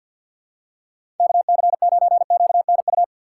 Morse code, heard as a keyed tone, about 51 wpm, 700 Hz. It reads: KC9QNU
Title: USA Call Signs - 4x Character Spacing - 50wpm